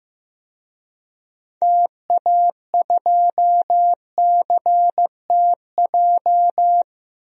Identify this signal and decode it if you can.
Morse code, heard as a keyed tone, about 15 wpm, 700 Hz. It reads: TA2CTJ
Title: DXCC-100 Call Signs - 6 Characters - Repeated 2x - 15wpm